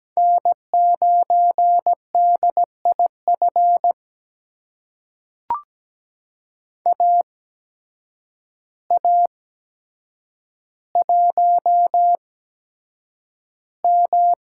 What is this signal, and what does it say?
Morse code, heard as a keyed tone, about 17 wpm, spaced as for 8 wpm, 700 Hz. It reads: N9DIF AA1M
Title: USA Call Signs - 5 Characters - 8x Character Spacing - 17wpm